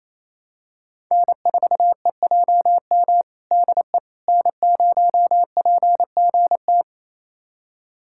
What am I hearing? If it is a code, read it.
Morse code, heard as a keyed tone, about 27 wpm, 700 Hz. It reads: N4EJM DE N0PGT
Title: USA Callsign DE Callsign - 5 Characters - 28wpm